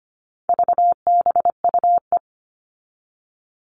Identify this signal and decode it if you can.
Morse code, heard as a keyed tone, about 25 wpm, 700 Hz. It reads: VBUE
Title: All Cut Numbers - 4-Digit 25wpm